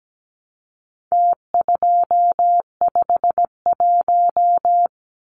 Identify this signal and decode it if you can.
Morse code, heard as a keyed tone, about 17 wpm, 700 Hz. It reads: T251
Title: Numbers - Contest Style - 4-Digit 17wpm